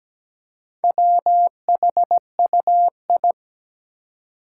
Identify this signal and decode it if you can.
Morse code, heard as a keyed tone, about 17 wpm, 700 Hz. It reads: WHUI
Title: Four Challenging Letter-Number Combinations 17wpm